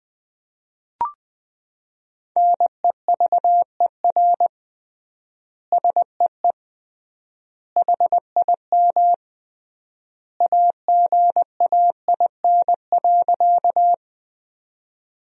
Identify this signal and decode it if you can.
Morse code, heard as a keyed tone, about 21 wpm, 700 Hz. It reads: NEVER SEE HIM AGAIN.
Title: 3x Word-Spacing - Sentences from Top 300 Words - 20wpm